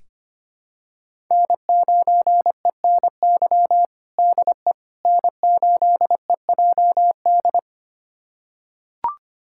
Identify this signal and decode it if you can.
Morse code, heard as a keyed tone, about 25 wpm, 700 Hz. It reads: N9ENY DE N8EJD
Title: USA Callsign DE Callsign - 5 Characters - 25wpm